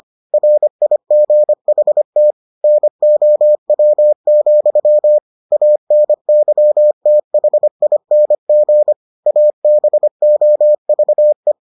Morse code, heard as a keyed tone, about 25 wpm, 600 Hz. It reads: RIGHT NOW, ANYTHING ABOVE